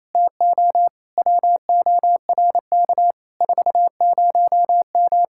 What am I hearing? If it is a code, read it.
Morse code, heard as a keyed tone, about 27 wpm, 700 Hz. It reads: TO WORK 40M
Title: Speed-Racing Common QSO Phrases - 28wpm